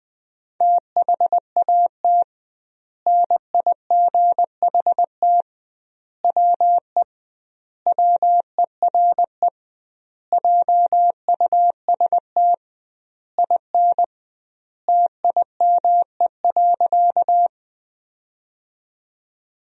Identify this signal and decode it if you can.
Morse code, heard as a keyed tone, about 20 wpm, 700 Hz. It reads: THAT NIGHT WE WERE JUST IN TIME.
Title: 2x Word-Spacing - Sentences from Top 300 Words - 20wpm